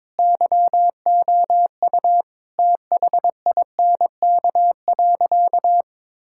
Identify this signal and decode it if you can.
Morse code, heard as a keyed tone, about 22 wpm, 700 Hz. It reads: YOU THINK.
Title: Sentences from Top 1000 Words - Review Part 6 - 22wpm